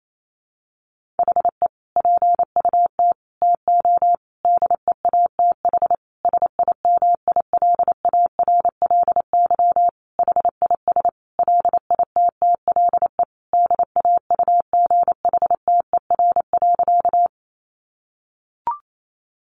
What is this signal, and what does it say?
Morse code, heard as a keyed tone, about 28 wpm, 700 Hz. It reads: HE PUT TO DEATH SIMILARLY HIS LITTLE DAUGHTER.
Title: Sentences from Top 1500 Words 28wpm